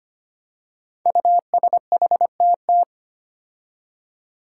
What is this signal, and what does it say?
Morse code, heard as a keyed tone, about 25 wpm, 700 Hz. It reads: USHTT
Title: Five Challenging Letter-Number Combinations 25wpm